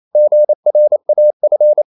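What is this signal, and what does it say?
Morse code, heard as a keyed tone, about 28 wpm, 600 Hz. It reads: GRAF
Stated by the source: Danish short words sent in Morse at 28 wpm, from CW med Gnister.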